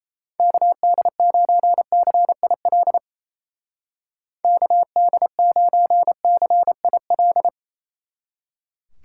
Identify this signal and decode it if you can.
Morse code, heard as a keyed tone, about 37 wpm, 700 Hz. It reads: KD9CIL KD9CIL